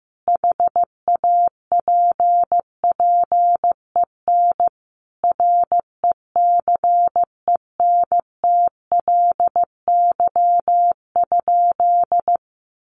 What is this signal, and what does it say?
Morse code, heard as a keyed tone, about 15 wpm, 700 Hz. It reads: HAPPEN RECENTLY?
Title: Sentences from Top 1200 Words 15wpm